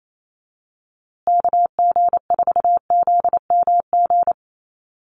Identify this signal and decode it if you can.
Morse code, heard as a keyed tone, about 28 wpm, 700 Hz. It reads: KG4ZMG
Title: USA Call Signs Repeated 2x 28wpm